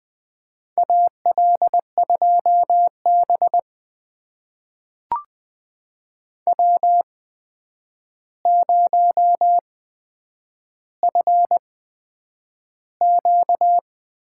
Morse code, heard as a keyed tone, about 20 wpm, spaced as for 9 wpm, 700 Hz. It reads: AL2B W0FQ
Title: USA Call Signs - 8x Character Spacing - 20wpm